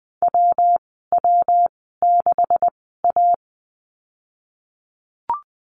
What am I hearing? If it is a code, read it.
Morse code, heard as a keyed tone, about 19 wpm, spaced as for 14 wpm, 700 Hz. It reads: WW6A
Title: 161 Lesson 26g - Letter g - Call Signs 20wpm